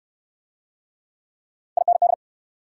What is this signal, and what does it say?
Morse code, heard as a keyed tone, about 64 wpm, 700 Hz. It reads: IND